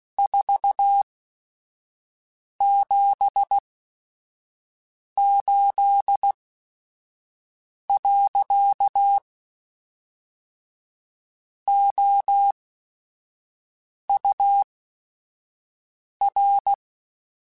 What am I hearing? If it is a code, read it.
Morse code, heard as a keyed tone, about 16 wpm, spaced as for 5 wpm, 800 Hz. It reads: 478. OUR